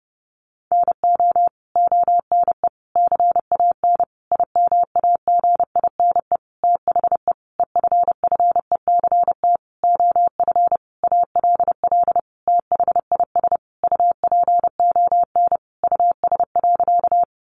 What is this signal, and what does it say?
Morse code, heard as a keyed tone, about 30 wpm, 700 Hz. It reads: NO ONE CAN IMAGINE THE EFFECT OF ALL THIS UPON US.
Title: Sentences from Top 1400 Words 30wpm